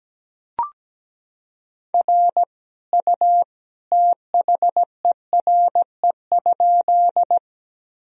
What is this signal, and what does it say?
Morse code, heard as a keyed tone, about 17 wpm, 700 Hz. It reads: R U THERE?
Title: Sentences from Top 100 Words and Abbreviations 17wpm